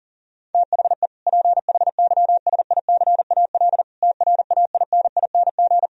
Morse code, heard as a keyed tone, about 40 wpm, 700 Hz. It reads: THE PHYSICAL TRAINING
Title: Sentences from Top 1000 Words - Review Part 4 - 40wpm